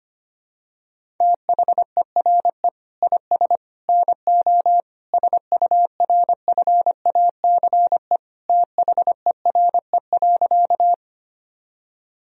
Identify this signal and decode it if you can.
Morse code, heard as a keyed tone, about 25 wpm, 700 Hz. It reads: THERE IS NO SURFACE THERE.
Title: Sentences from Top 1000 Words - Review Part 3 - 25wpm